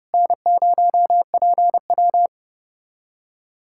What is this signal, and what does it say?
Morse code, heard as a keyed tone, about 30 wpm, 700 Hz. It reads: N0PW
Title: USA Callsign DE Callsign - 4 Characters - 30wpm